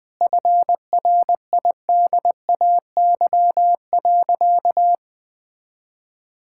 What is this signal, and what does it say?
Morse code, heard as a keyed tone, about 21 wpm, 700 Hz. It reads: FRIDAY.
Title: Sentences with 7-Word Phrases - 20wpm